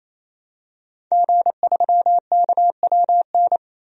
Morse code, heard as a keyed tone, about 28 wpm, 700 Hz. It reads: G3KWN
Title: DXCC-100 Call Signs - 5 Characters - 28wpm